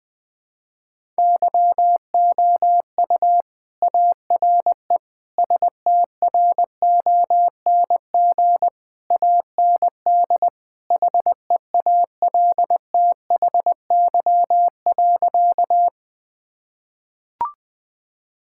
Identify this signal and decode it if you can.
Morse code, heard as a keyed tone, about 20 wpm, 700 Hz. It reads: YOU ARE STRONG AND HEALTHY.
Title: Sentences from Top 1000 Words - Review Part 5 - 20wpm